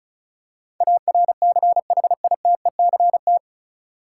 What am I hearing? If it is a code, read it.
Morse code, heard as a keyed tone, about 34 wpm, 700 Hz. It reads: ARCHITECT
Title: Practice QSO - Exchange 3 - More Info - 35wpm